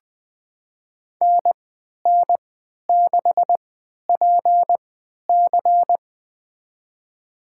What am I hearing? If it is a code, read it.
Morse code, heard as a keyed tone, about 20 wpm, spaced as for 11 wpm, 700 Hz. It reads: NN6PC
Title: USA Call Signs - 3x Character Spacing - 20wpm